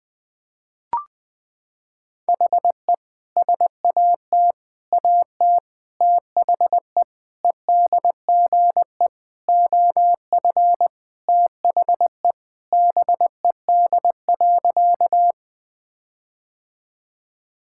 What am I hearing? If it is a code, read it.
Morse code, heard as a keyed tone, about 20 wpm, 700 Hz. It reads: HE SAT AT THE EDGE OF THE BED.
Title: Sentences with 5-Word Phrases - 20wpm